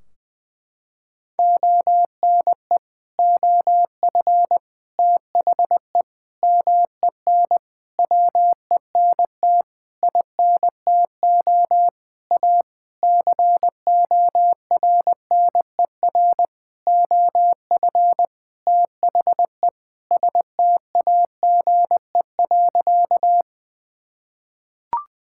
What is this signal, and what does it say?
Morse code, heard as a keyed tone, about 21 wpm, 700 Hz. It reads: ONE OF THE MEN WENT INTO A CORNER OF THE STAGE.
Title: Sentences from Top 1600 Words 20wpm